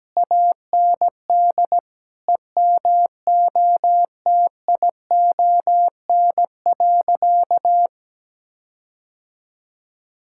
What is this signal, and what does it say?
Morse code, heard as a keyed tone, about 17 wpm, 700 Hz. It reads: AND EMOTION.
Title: Sentences from Top 1600 Words 17wpm